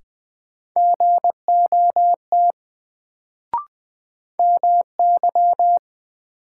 Morse code, heard as a keyed tone, about 20 wpm, 700 Hz. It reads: GOT MY
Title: Sets of 2 Words - New Word Drill - 20wpm